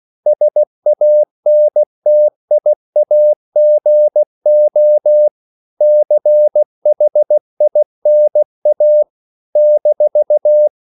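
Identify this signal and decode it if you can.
Morse code, heard as a keyed tone, about 16 wpm, 600 Hz. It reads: SANTIAGO CHINA -
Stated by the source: CW practice of national capitals at 16 words per minute, from ON6ZQ.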